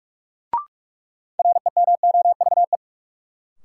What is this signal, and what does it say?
Morse code, heard as a keyed tone, about 44 wpm, 700 Hz. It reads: REMOVE
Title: Sentences from Top 800 Words - New Word Drill - 45wpm